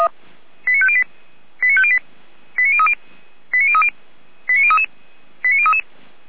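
Five-tone selective call (ZVEI), heard as quick runs of five tones, tones of about 70 ms. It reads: 89508 89508 89030 89030 89033 89033
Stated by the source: ZVEI five-tone selective call received by SDR.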